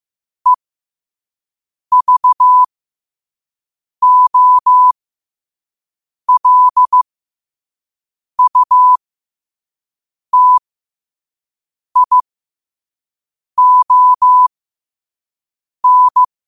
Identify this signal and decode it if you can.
Morse code, heard as a keyed tone, about 15 wpm, spaced as for 5 wpm, 1000 Hz. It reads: EVOLUTION